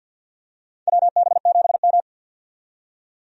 Morse code, heard as a keyed tone, about 50 wpm, 700 Hz. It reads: WB7M
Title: USA Call Signs - 4 Characters - 50wpm